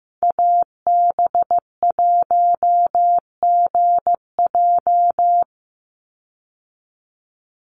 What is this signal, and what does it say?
Morse code, heard as a keyed tone, about 15 wpm, 700 Hz. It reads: AB1GJ
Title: Call Signs Repeated 3x - 15wpm